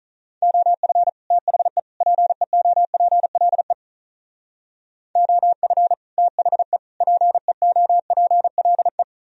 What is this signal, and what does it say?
Morse code, heard as a keyed tone, about 45 wpm, 700 Hz. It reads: OF THE PEOPLE OF THE PEOPLE